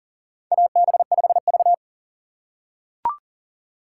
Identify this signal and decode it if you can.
Morse code, heard as a keyed tone, about 40 wpm, 700 Hz. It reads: AB5V
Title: USA Call Signs - 4 Characters - 4x Character Spacing - 40wpm